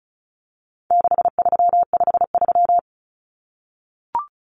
Morse code, heard as a keyed tone, about 34 wpm, 700 Hz. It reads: 6353
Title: Numbers - 4-Digit 35wpm